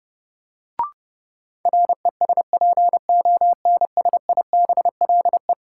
Morse code, heard as a keyed tone, about 31 wpm, 700 Hz. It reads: RESPONSIBLE
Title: Sentences from Top 1600 Words - New Word Drill Alt - 30wpm